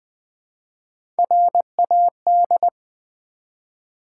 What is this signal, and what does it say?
Morse code, heard as a keyed tone, about 20 wpm, 700 Hz. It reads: RAD
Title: Build A Word - 1 Letter At A Time - 5-Letter Words - 20wpm